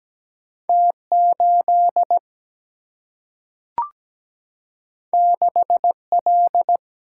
Morse code, heard as a keyed tone, about 17 wpm, 700 Hz. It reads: T8 6L
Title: Single-Letter Single-Number 17wpm